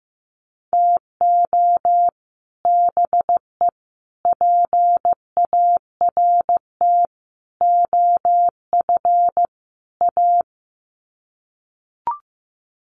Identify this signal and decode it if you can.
Morse code, heard as a keyed tone, about 15 wpm, 700 Hz. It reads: TO BE PART OF A
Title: Sets of 5 Words Encore2 15wpm